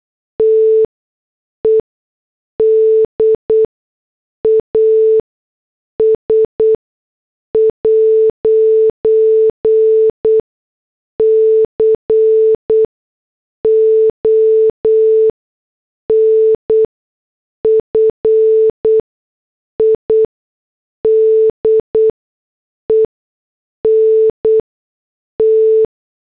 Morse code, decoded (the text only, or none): TED AS 'CONFIDENT